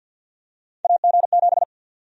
Morse code, heard as a keyed tone, about 51 wpm, 700 Hz. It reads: AG7
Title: USA Call Sign Prefixes - Repeated 2x - 50wpm